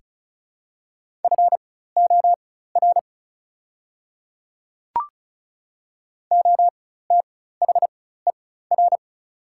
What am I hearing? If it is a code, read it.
Morse code, heard as a keyed tone, about 34 wpm, spaced as for 16 wpm, 700 Hz. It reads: FOR OTHER
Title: 4x Character Spacing - Top 100 Words - 35wpm